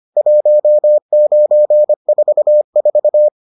Morse code, heard as a keyed tone, about 25 wpm, 600 Hz. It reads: 1944